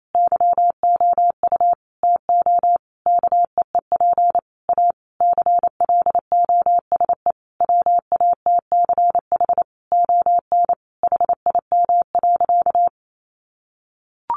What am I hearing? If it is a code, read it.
Morse code, heard as a keyed tone, about 28 wpm, 700 Hz. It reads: YOU TO KEEP A CLOSE WATCH ON HIM.
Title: Sentences from Top 1300 Words 28wpm